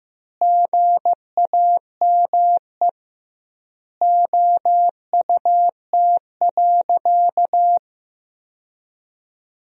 Morse code, heard as a keyed tone, about 14 wpm, 700 Hz. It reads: GAME OUT.
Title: 2x Word-Spacing - Sentences from Top 500 Words - 15wpm